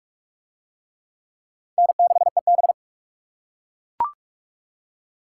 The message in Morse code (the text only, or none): N6EB